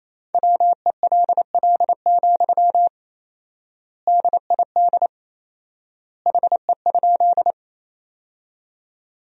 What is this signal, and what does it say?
Morse code, heard as a keyed tone, about 28 wpm, 700 Hz. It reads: WELL, DID HE?
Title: 4x Word-Spacing - Sentences from Top 200 Words - 28wpm